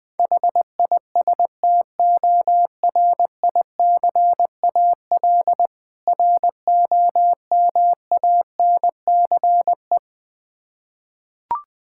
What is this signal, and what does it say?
Morse code, heard as a keyed tone, about 20 wpm, 700 Hz. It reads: HISTORICAL ROMANCE
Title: Difficult Words with Hints - 20wpm